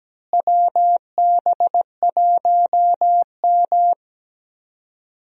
Morse code, heard as a keyed tone, about 17 wpm, 700 Hz. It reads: WB1M
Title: USA Call Signs - 4 Characters - Repeated 2x - 17wpm